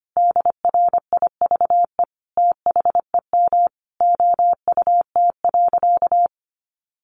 Morse code, heard as a keyed tone, about 25 wpm, 700 Hz. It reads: DRIVE THEM OUT.
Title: Sentences from Top 500 Words 25wpm